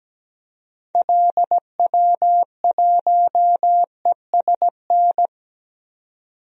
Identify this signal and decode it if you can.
Morse code, heard as a keyed tone, about 17 wpm, 700 Hz. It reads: LW1ESN
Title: DXCC-100 Call Signs - 6 Characters - 17wpm